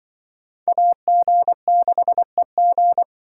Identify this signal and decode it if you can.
Morse code, heard as a keyed tone, about 25 wpm, 700 Hz. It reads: AG6EG
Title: Speed Racing - USA Call Signs 20wpm